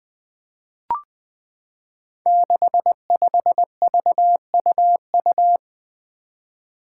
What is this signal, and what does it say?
Morse code, heard as a keyed tone, about 20 wpm, 700 Hz. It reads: 65VUU
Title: Five Challenging Letter-Number Combinations 20wpm